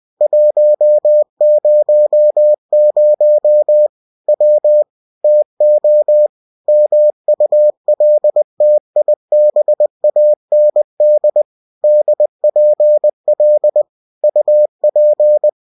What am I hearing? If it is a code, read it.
Morse code, heard as a keyed tone, about 20 wpm, 600 Hz. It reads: 100 W TO MULTIBAND DPL UP